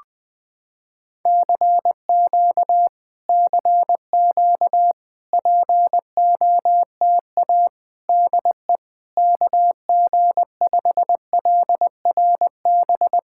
Morse code, heard as a keyed tone, about 20 wpm, 700 Hz. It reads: CQ CQ POTA DE KG5LRB
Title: POTA QSO Practice - 20wpm